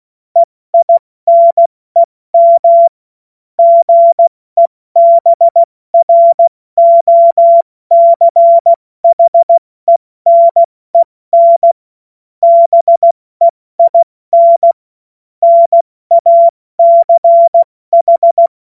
Morse code, heard as a keyed tone, about 16 wpm, 680 Hz. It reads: EINEM GEBROCHENEN BEIN NACH
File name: Morsekurs / Witze / Witz 21. September 2021, 70 F80